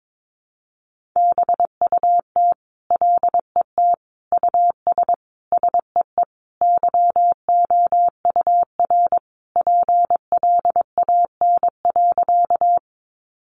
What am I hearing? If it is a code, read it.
Morse code, heard as a keyed tone, about 22 wpm, 700 Hz. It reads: BUT LET US SEE YOUR PLAN.
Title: Sentences from Top 500 Words 22wpm